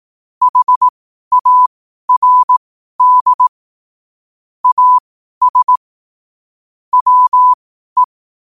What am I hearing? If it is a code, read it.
Morse code, heard as a keyed tone, about 18 wpm, spaced as for 12 wpm, 1000 Hz. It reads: HARD AS WE
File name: qotdfiles / 12wpm